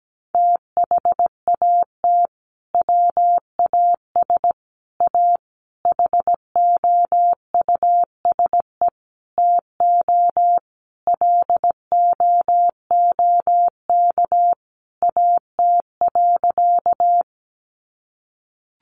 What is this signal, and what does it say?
Morse code, heard as a keyed tone, about 17 wpm, 700 Hz. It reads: THAT WAS A HOUSE TO LOOK AT.
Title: Sentences from Top 200 Words 17wpm